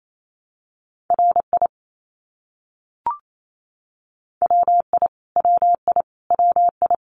Morse code, heard as a keyed tone, about 28 wpm, 700 Hz. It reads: RI WI WI WI
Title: US States Repeated 3x 28wpm